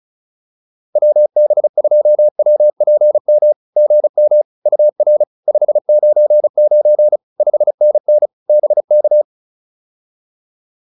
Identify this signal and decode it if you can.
Morse code, heard as a keyed tone, about 34 wpm, 600 Hz. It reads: WB2WPM GM UR 599 5NN BK
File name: POTA210 35